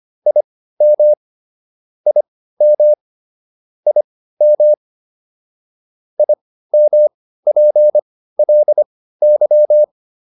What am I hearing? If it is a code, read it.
Morse code, heard as a keyed tone, about 25 wpm, spaced as for 15 wpm, 600 Hz. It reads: IM IM IM IMPLY